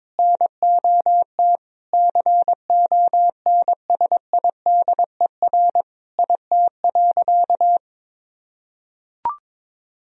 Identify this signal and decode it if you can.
Morse code, heard as a keyed tone, about 21 wpm, 700 Hz. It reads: NOT CONSIDER IT.
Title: Sentences from Top 1000 Words - Review Part 4 - 22wpm